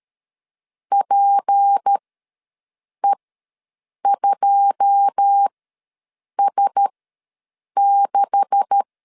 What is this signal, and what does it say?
Morse code, heard as a keyed tone, about 13 wpm, spaced as for 7 wpm, 800 Hz. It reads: PE2S6
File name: LEZIONI / 10 SIGNS / 10A SLASH ws 12wpm 3443